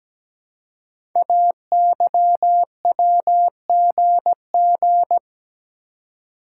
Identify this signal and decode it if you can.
Morse code, heard as a keyed tone, about 17 wpm, 700 Hz. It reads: AYWGG
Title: Code Groups - 5 Letters - 17wpm